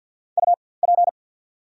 Morse code, heard as a keyed tone, about 54 wpm, spaced as for 24 wpm, 690 Hz. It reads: UP
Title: 4x Character Spacing - Top 100 Words - 50wpm